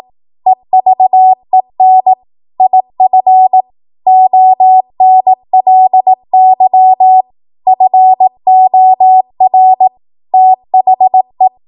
Morse code, heard as a keyed tone, about 18 wpm, 750 Hz. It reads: EVEN IF ONLY FOR THE
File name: Morse / 18WPM